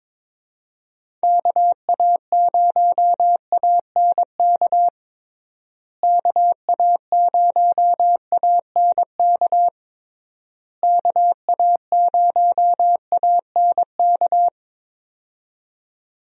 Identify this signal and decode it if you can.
Morse code, heard as a keyed tone, about 22 wpm, 700 Hz. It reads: KA0ANK KA0ANK KA0ANK
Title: USA Call Signs - 6 Characters - Repeated 3x - 22wpm